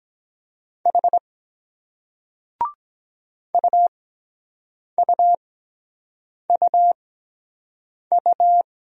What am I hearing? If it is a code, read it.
Morse code, heard as a keyed tone, about 23 wpm, spaced as for 7 wpm, 700 Hz. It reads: H UUUU